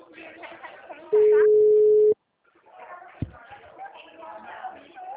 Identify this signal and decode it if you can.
Ringback tone (Europe, 425 Hz), with speech.
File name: out- -2009- - - .174